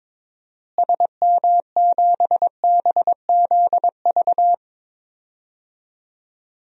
Morse code, heard as a keyed tone, about 22 wpm, 700 Hz. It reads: SM7BZV
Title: DXCC-100 Call Signs - 6 Characters - Repeated 3x - 22wpm